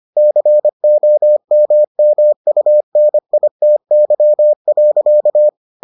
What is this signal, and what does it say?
Morse code, heard as a keyed tone, about 25 wpm, 600 Hz. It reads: COMMUNITY.